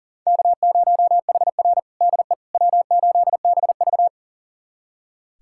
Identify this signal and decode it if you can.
Morse code, heard as a keyed tone, about 40 wpm, 700 Hz. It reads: K0HR DE W8BV
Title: USA Callsign DE Callsign - 4 Characters - 40wpm